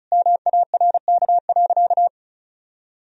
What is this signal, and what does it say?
Morse code, heard as a keyed tone, about 36 wpm, 700 Hz. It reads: MARK.
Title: Sentences from Top 1000 Words - Review Part 1 - 35wpm